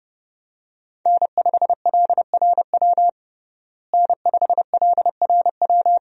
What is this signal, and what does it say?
Morse code, heard as a keyed tone, about 32 wpm, 700 Hz. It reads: N5LRW N5LRW
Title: USA Call Signs - 5 Characters - Repeated 2x - 30wpm